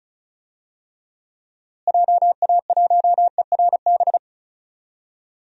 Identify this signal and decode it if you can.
Morse code, heard as a keyed tone, about 36 wpm, 700 Hz. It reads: JA1ERB
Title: DXCC-100 Call Signs - 6 Characters - Repeated 3x - 35wpm